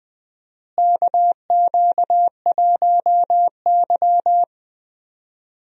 Morse code, heard as a keyed tone, about 21 wpm, 700 Hz. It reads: KQ1Y
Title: USA Call Signs - 4 Characters - Repeated 2x - 20wpm